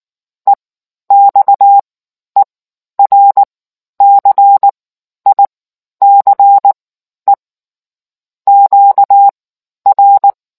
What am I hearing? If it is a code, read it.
Morse code, heard as a keyed tone, about 19 wpm, spaced as for 11 wpm, 800 Hz. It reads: EXERCICE QR